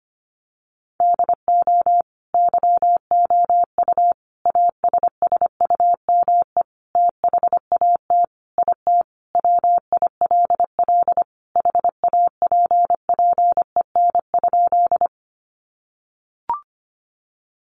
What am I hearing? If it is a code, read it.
Morse code, heard as a keyed tone, about 25 wpm, 700 Hz. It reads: DO YOU ASSUME THAT IT WILL HAPPEN?
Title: Sentences from Top 1500 Words 25wpm